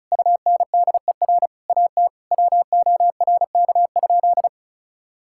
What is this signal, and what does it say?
Morse code, heard as a keyed tone, about 34 wpm, 700 Hz. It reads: UNDER AT WORK?
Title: Sentences from Top 1000 Words - Review Part 6 - 35wpm